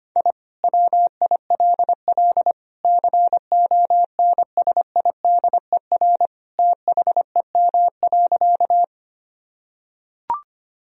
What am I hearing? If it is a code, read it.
Morse code, heard as a keyed tone, about 25 wpm, 700 Hz. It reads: I WILL CONSIDER THEM.
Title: Sentences from Top 1000 Words - Review Part 4 - 25wpm